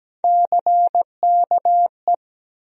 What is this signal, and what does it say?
Morse code, heard as a keyed tone, about 17 wpm, 700 Hz. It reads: CKE
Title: USA Call Sign Postfixes - 3 Letters 17wpm